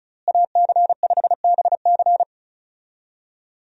Morse code, heard as a keyed tone, about 36 wpm, 700 Hz. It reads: AC5BC
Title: USA Call Signs - 5 Characters - Repeated 2x - 35wpm